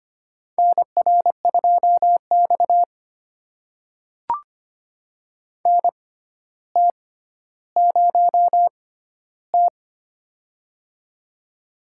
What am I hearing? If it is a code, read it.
Morse code, heard as a keyed tone, about 25 wpm, spaced as for 13 wpm, 700 Hz. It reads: NR2X NT0T
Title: USA Call Signs - 4 Characters - 6x Character Spacing - 25wpm